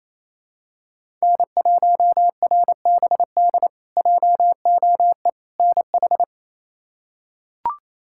Morse code, heard as a keyed tone, about 28 wpm, 700 Hz. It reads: N1RBD JOE NH
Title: K1USN SST QSO Practice - 28wpm